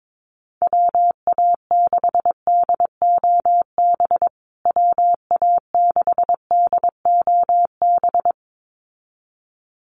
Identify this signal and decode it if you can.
Morse code, heard as a keyed tone, about 22 wpm, 700 Hz. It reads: WA6DOB WA6DOB
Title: USA Call Signs Repeated 2x 22wpm